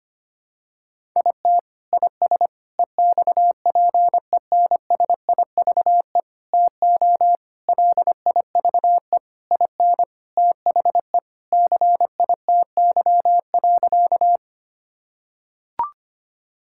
Morse code, heard as a keyed tone, about 25 wpm, 700 Hz. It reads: IT IS EXPENSIVE TO LIVE IN THE CITY.
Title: Sentences from Top 1000 Words - Review Part 6 - 25wpm